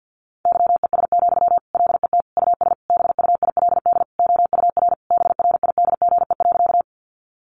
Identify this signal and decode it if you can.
Morse code, heard as a keyed tone, about 50 wpm, 700 Hz. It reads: YES, LET US BUILD OUR BRIDGE.
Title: Sentences from Top 1200 Words 50wpm